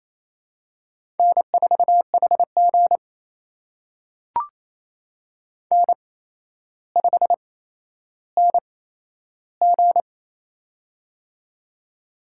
Morse code, heard as a keyed tone, about 27 wpm, spaced as for 12 wpm, 700 Hz. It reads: N4HG N5NG
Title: USA Call Signs - 4 Characters - 8x Character Spacing - 28wpm